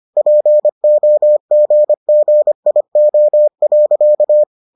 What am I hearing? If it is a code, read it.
Morse code, heard as a keyed tone, about 25 wpm, 600 Hz. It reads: POGGIO.